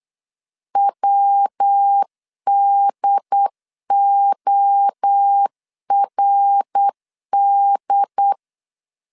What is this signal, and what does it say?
Morse code, heard as a keyed tone, about 8 wpm, 800 Hz. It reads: WDORD
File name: 8wpm